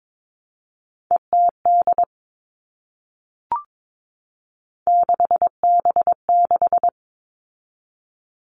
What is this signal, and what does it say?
Morse code, heard as a keyed tone, about 22 wpm, 700 Hz. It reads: ETD 6B6
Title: All Cut Numbers - 3-Digit 22wpm